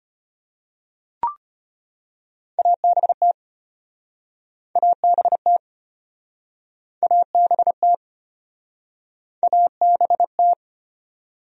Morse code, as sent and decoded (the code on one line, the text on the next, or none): .- -... - / .- -... - / .- -... - / .- -... -
ABT ABT ABT ABT